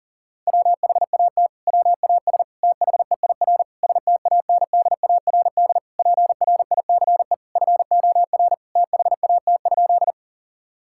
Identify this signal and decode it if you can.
Morse code, heard as a keyed tone, about 40 wpm, 700 Hz. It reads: WHAT WAS THEIR STANDARD PRICE FOR THAT?
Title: Sentences from Top 1000 Words - Review Part 6 - 40wpm